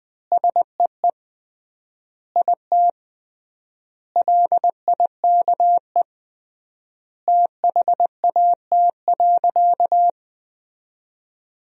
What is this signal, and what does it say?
Morse code, heard as a keyed tone, about 20 wpm, 700 Hz. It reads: SEE IT LIKE THAT.
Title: 3x Word-Spacing - Sentences from Top 300 Words - 20wpm